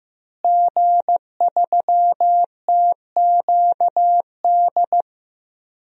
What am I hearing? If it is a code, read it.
Morse code, heard as a keyed tone, about 15 wpm, 700 Hz. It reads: G3TQD
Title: DXCC-100 Call Signs - 5 Characters - Repeated 3x - 15wpm